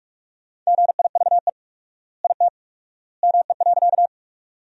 Morse code, heard as a keyed tone, about 43 wpm, 700 Hz. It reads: GIVE IT ME.
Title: 4x Word-Spacing - Sentences from Top 200 Words - 45wpm